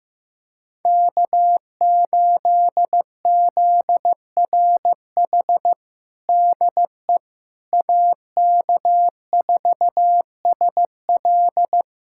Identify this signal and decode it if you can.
Morse code, heard as a keyed tone, about 15 wpm, 700 Hz. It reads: K8ZRH DE AK4SL
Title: USA Callsign DE Callsign - 5 Characters - 15wpm